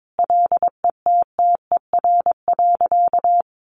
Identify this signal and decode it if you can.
Morse code, heard as a keyed tone, about 22 wpm, 700 Hz. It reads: LETTER.
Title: A Sound Of Thunder 22wpm